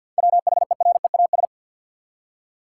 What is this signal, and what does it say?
Morse code, heard as a keyed tone, about 52 wpm, 700 Hz. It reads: WHEREAS
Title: Seven Letter Words - 50wpm